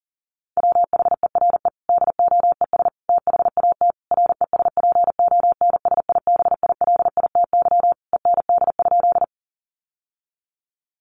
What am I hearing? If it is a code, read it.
Morse code, heard as a keyed tone, about 40 wpm, 700 Hz. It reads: WHERE DOES THAT RESPONSIBILITY END?
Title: Sentences from Top 1200 Words 40wpm